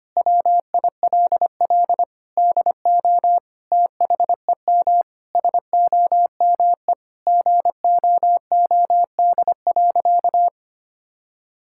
Morse code, heard as a keyed tone, about 25 wpm, 700 Hz. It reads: WILL DO THEM SOME GOOD.
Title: Sentences from Top 1000 Words - Review Part 6 - 25wpm